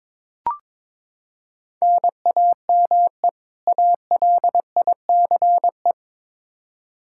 Morse code, heard as a keyed tone, about 21 wpm, 700 Hz. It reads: NAME ALICE
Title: Practice QSO - Exchange 1 - Beginning - 22wpm